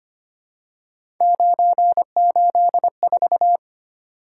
Morse code, heard as a keyed tone, about 25 wpm, 700 Hz. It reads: 984
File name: Three Letter-Number Combinations - 25wpm